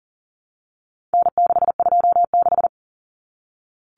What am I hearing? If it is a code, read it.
Morse code, heard as a keyed tone, about 40 wpm, 700 Hz. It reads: N626
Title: Common Cut Numbers - 4-Digit 40wpm